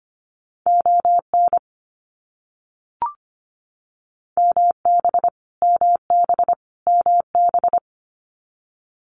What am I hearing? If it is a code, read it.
Morse code, heard as a keyed tone, about 25 wpm, 700 Hz. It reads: ON MB MB MB
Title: Canadian Provinces and Territories - Repeated 3x - 25wpm